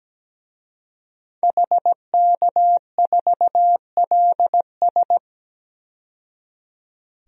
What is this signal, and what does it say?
Morse code, heard as a keyed tone, about 17 wpm, 700 Hz. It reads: HK4LS
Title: DXCC-100 Call Signs - 5 Characters - 17wpm